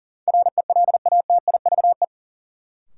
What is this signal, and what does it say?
Morse code, heard as a keyed tone, about 40 wpm, 700 Hz. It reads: RELATIVE
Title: Sentences from Top 1400 Words - New Word Drill - 40wpm